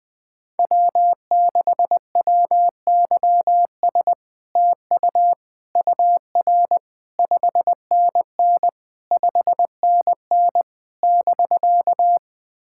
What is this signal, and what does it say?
Morse code, heard as a keyed tone, about 20 wpm, 700 Hz. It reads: W6WYS TU UR 5NN 5NN <BK>
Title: SOTA QSO Practice 20wpm